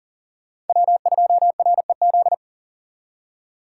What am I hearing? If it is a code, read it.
Morse code, heard as a keyed tone, about 52 wpm, 700 Hz. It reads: W2REZ